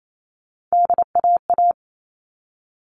Morse code, heard as a keyed tone, about 28 wpm, 700 Hz. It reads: DAA
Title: All Cut Numbers - 3-Digit 28wpm